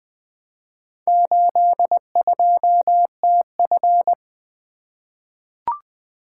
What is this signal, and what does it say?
Morse code, heard as a keyed tone, about 21 wpm, 700 Hz. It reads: 82TF
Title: Four Letter-Number Combinations - 20wpm